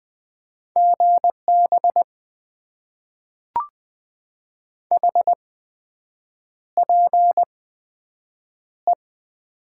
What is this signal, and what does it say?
Morse code, heard as a keyed tone, about 20 wpm, spaced as for 7 wpm, 700 Hz. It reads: GB HPE
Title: Top 90 QSO Elements - 8x Character Spacing - 20wpm